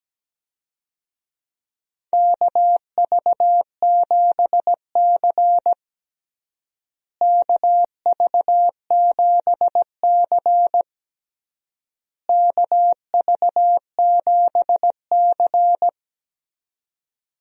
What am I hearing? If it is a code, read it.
Morse code, heard as a keyed tone, about 17 wpm, 700 Hz. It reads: KV7C KV7C KV7C
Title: USA Call Signs - 4 Characters - Repeated 3x - 17wpm